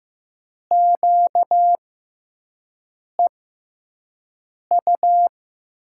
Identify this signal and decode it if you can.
Morse code, heard as a keyed tone, about 15 wpm, spaced as for 5 wpm, 700 Hz. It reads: QEU
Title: USA Call Signs - 6 Characters - 6x Character Spacing - 15wpm